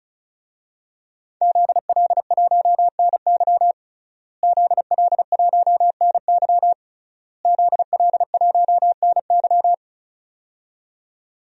Morse code, heard as a keyed tone, about 34 wpm, 690 Hz. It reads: ZL1NY ZL1NY ZL1NY
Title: DXCC-100 Call Signs - 5 Characters - Repeated 3x - 35wpm